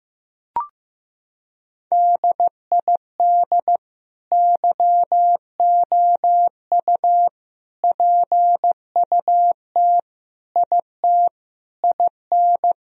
Morse code, heard as a keyed tone, about 16 wpm, 700 Hz. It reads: DID YOU PUT IT IN
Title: Sentences from Top 1000 Words - Review Part 4 - 15wpm